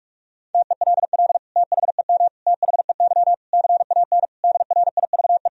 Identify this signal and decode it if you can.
Morse code, heard as a keyed tone, about 41 wpm, 700 Hz. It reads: TELL THEM THEY CAN DRIVE
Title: Sentences from Top 1000 Words - Review Part 2 - 45wpm